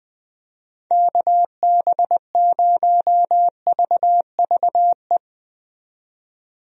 Morse code, heard as a keyed tone, about 21 wpm, 700 Hz. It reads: KB0VVE
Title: USA Call Signs - 6 Characters - 20wpm